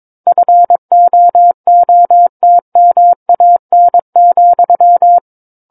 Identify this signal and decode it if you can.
Morse code, heard as a keyed tone, about 23 wpm, 700 Hz. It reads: FOOTMAN,
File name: 06-Chapter-06 22 WPM